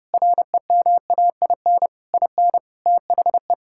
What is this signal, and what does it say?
Morse code, heard as a keyed tone, about 32 wpm, 700 Hz. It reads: REMAIN IN THE
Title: Sentences from Top 1600 Words 30wpm